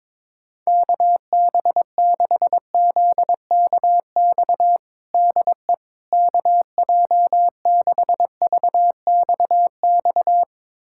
Morse code, heard as a keyed tone, about 22 wpm, 700 Hz. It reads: KB6ZKX DE KJ6VXX
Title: USA Callsign DE Callsign - 6 Characters - 22wpm